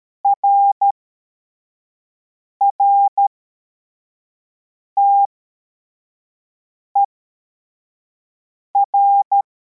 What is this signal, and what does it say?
Morse code, heard as a keyed tone, about 13 wpm, spaced as for 4 wpm, 800 Hz. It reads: RRTER